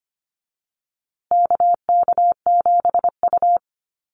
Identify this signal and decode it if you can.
Morse code, heard as a keyed tone, about 25 wpm, 700 Hz. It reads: KK7U
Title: USA Call Signs Repeated 2x 25wpm